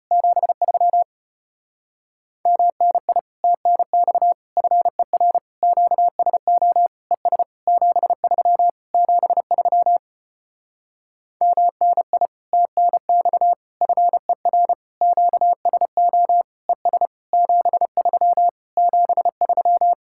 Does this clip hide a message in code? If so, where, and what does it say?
Yes: Morse code from 0.1 to 19.9 s, heard as a keyed tone, about 33 wpm, 700 Hz. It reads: 73 MNI TNX FER QSO ES 73 73 MNI TNX FER QSO ES 73 73